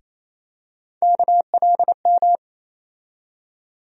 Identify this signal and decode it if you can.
Morse code, heard as a keyed tone, about 28 wpm, 700 Hz. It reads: KLM
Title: USA Call Sign Postfixes - 3 Letters 28wpm